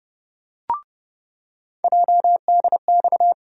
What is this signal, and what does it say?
Morse code, heard as a keyed tone, about 30 wpm, 700 Hz. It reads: JDX
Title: USA Call Sign Postfixes - 3 Letters 30wpm